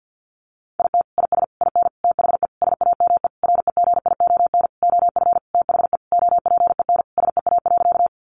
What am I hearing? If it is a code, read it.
Morse code, heard as a keyed tone, about 50 wpm, 700 Hz. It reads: IT IS IN THE SAME REGION OF THE OPEN SEA.
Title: Sentences from Top 1500 Words 50wpm